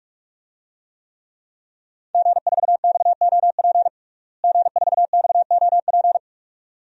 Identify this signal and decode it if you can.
Morse code, heard as a keyed tone, about 46 wpm, 700 Hz. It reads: G4XOP G4XOP
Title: DXCC-100 Call Signs - 5 Characters - Repeated 2x - 45wpm